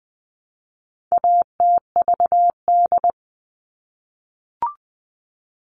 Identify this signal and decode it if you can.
Morse code, heard as a keyed tone, about 20 wpm, 700 Hz. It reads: ATVD
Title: All Cut Numbers - 4-Digit 20wpm